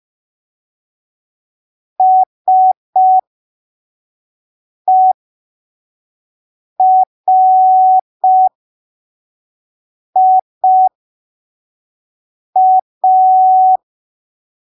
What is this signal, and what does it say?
Morse code, heard as a keyed tone, about 5 wpm, spaced as for 3 wpm, 750 Hz. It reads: SERIA
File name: arquivos / texto / texto 15 05ppm